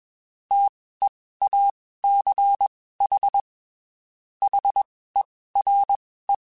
Morse code, heard as a keyed tone, about 21 wpm, spaced as for 15 wpm, 800 Hz. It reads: TEACH HERE
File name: (spaced) / QSOs Spaced-mp3s / WD0BGP 1 @20 sp